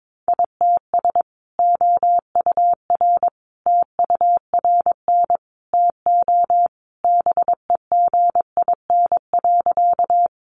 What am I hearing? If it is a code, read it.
Morse code, heard as a keyed tone, about 22 wpm, 700 Hz. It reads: ITS OUR TURN TO BEGIN.
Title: Sentences from Top 600 Words 22wpm